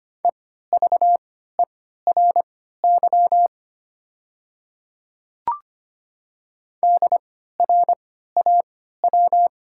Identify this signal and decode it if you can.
Morse code, heard as a keyed tone, about 25 wpm, spaced as for 14 wpm, 700 Hz. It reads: EVERY DRAW
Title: 3x Character Spacing - Top 300 Words - 25wpm